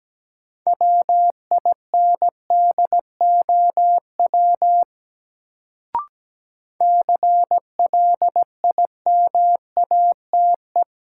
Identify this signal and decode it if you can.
Morse code, heard as a keyed tone, about 17 wpm, 700 Hz. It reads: WINDOW CLIMATE
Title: Sentences from Top 1600 Words - New Word Drill Alt - 17wpm